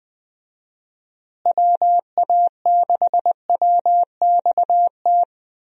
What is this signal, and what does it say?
Morse code, heard as a keyed tone, about 21 wpm, 700 Hz. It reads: WA6WXT
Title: USA Call Signs - 6 Characters - Repeated 2x - 20wpm